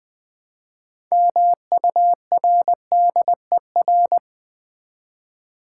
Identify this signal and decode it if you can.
Morse code, heard as a keyed tone, about 20 wpm, 700 Hz. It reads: MURDER
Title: Build A Word - 3 Letters At A Time - 6-Letter Words - 20wpm